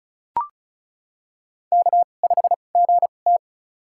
Morse code, heard as a keyed tone, about 36 wpm, spaced as for 25 wpm, 700 Hz. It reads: K5GT
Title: USA Call Signs - 4 Characters - 2x Character Spacing - 35wpm